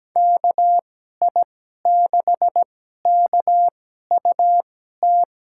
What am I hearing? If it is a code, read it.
Morse code, heard as a keyed tone, about 17 wpm, spaced as for 12 wpm, 700 Hz. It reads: KI6KUT
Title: USA Call Signs - 6 Characters - 2x Character Spacing - 17wpm